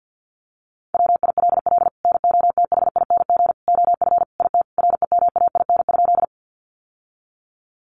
Morse code, heard as a keyed tone, about 50 wpm, 700 Hz. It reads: WILL NOTHING OF IT REMAIN?
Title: Sentences from Top 1200 Words 50wpm